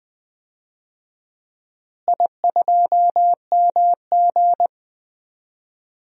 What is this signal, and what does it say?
Morse code, heard as a keyed tone, about 20 wpm, 700 Hz. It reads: I2MG
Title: DXCC-100 Call Signs - 4 Characters - Repeated 2x - 20wpm